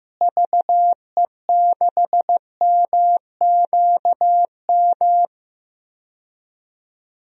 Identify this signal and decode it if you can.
Morse code, heard as a keyed tone, about 15 wpm, 700 Hz. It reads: VE6MQM
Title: DXCC-100 Call Signs - 6 Characters - Repeated 2x - 15wpm